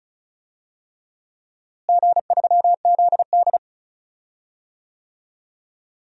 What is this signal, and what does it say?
Morse code, heard as a keyed tone, about 34 wpm, 700 Hz. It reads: G3ZD
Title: DXCC-100 Call Signs - 4 Characters - Repeated 3x - 35wpm